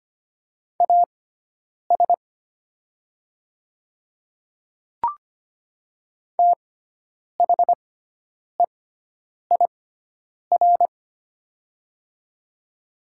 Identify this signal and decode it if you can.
Morse code, heard as a keyed tone, about 25 wpm, spaced as for 9 wpm, 700 Hz. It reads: AS THEIR
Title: 6x Character Spacing - Top 100 Words - 25wpm